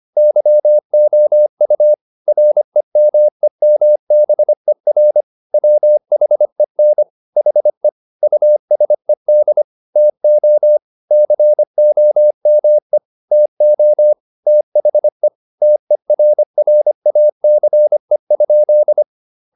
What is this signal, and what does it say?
Morse code, heard as a keyed tone, about 25 wpm, 600 Hz. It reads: YOU REMEMBER WHEN HE USED TO COME TO THE TERRACE?
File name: the old man and the sea part 1 25wpm